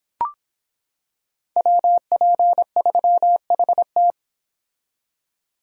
Morse code, heard as a keyed tone, about 26 wpm, 700 Hz. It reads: WP3HT